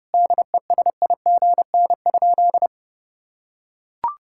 Morse code, heard as a keyed tone, about 32 wpm, 700 Hz. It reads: DESIGN?
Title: Sentences from Top 1000 Words - Review Part 5 - 30wpm